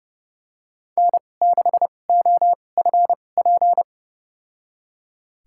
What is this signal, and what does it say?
Morse code, heard as a keyed tone, about 30 wpm, spaced as for 22 wpm, 700 Hz. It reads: N6OFP
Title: USA Call Signs - 5 Characters - 2x Character Spacing - 30wpm